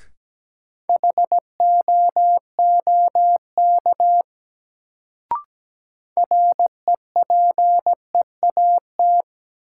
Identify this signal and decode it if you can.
Morse code, heard as a keyed tone, about 17 wpm, 700 Hz. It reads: HOOK REPEAT
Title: Sentences from Top 1600 Words - New Word Drill Alt - 17wpm